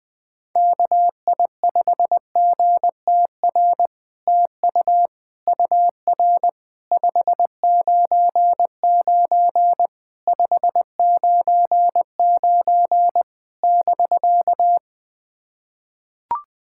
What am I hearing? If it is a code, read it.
Morse code, heard as a keyed tone, about 20 wpm, 700 Hz. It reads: KI5GTR TU UR 599 599 <BK>
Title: POTA QSO Practice - 20wpm